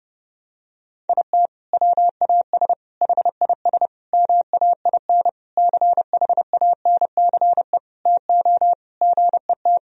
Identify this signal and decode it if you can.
Morse code, heard as a keyed tone, about 32 wpm, 700 Hz. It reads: IT WAS HIS MAIN CHANCE TO GET
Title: Sentences from Top 1000 Words - Review Part 2 - 30wpm